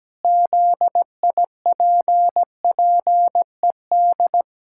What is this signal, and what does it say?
Morse code, heard as a keyed tone, about 17 wpm, 700 Hz. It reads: ZIPPED
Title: Pangram Challenge v2 - 17wpm